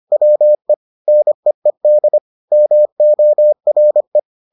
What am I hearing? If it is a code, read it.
Morse code, heard as a keyed tone, about 25 wpm, 600 Hz. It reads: WE NEED MORE